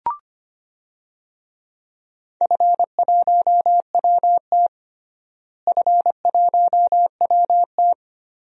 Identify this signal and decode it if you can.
Morse code, heard as a keyed tone, about 25 wpm, 700 Hz. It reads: F1WT F1WT
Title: DXCC-100 Call Signs - 4 Characters - Repeated 3x - 25wpm